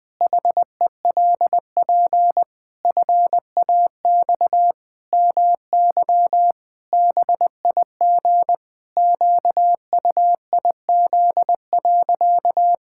Morse code, heard as a keyed tone, about 20 wpm, 700 Hz. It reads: HELP FAX MY BIG QUIZ.
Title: Pangram Challenge v2 - 20wpm